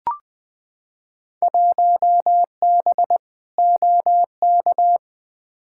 Morse code, heard as a keyed tone, about 20 wpm, 700 Hz. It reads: 1B OK
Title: Field Day Exchanges Encore - 20wpm